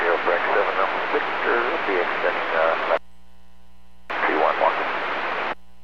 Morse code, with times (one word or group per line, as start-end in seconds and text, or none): none